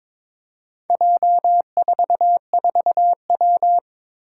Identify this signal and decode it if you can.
Morse code, heard as a keyed tone, about 21 wpm, 700 Hz. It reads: J44W
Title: Four Challenging Letter-Number Combinations 22wpm